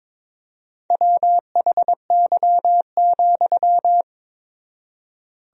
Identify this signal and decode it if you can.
Morse code, heard as a keyed tone, about 22 wpm, 700 Hz. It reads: WHY,
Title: 4x Word-Spacing - Sentences from Top 300 Words - 22wpm